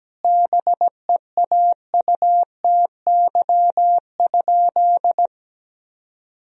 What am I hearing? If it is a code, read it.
Morse code, heard as a keyed tone, about 17 wpm, 700 Hz. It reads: BEAUTY?
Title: Sentences from Top 1000 Words - Review Part 2 - 17wpm